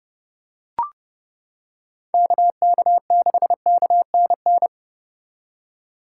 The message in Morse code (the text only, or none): KK6KNN